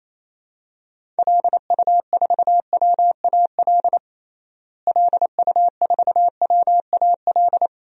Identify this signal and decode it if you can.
Morse code, heard as a keyed tone, about 28 wpm, 700 Hz. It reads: LU4WAL LU4WAL
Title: DXCC-100 Call Signs - 6 Characters - Repeated 3x - 28wpm